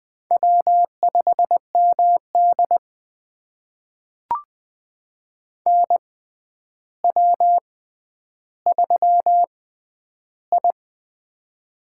Morse code, heard as a keyed tone, about 20 wpm, spaced as for 10 wpm, 700 Hz. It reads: W5MD NW3I
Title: USA Call Signs - 4 Characters - 6x Character Spacing - 20wpm